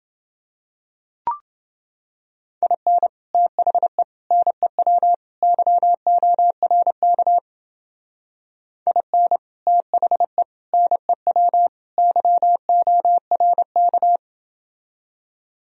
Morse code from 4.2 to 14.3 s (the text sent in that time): NEW YORK IN THE NEW YORK